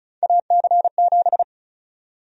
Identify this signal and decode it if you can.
Morse code, heard as a keyed tone, about 36 wpm, 700 Hz. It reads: AC7
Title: USA Call Sign Prefixes - 35wpm